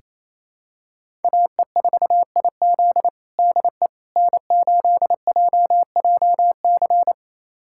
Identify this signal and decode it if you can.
Morse code, heard as a keyed tone, about 28 wpm, 700 Hz. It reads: AE4IZ DE N8JJC
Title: USA Callsign DE Callsign - 5 Characters - 28wpm